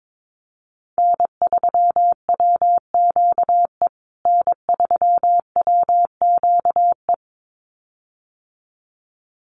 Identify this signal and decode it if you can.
Morse code, heard as a keyed tone, about 22 wpm, 700 Hz. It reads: N3WQE N3WQE
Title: USA Call Signs Repeated 2x 22wpm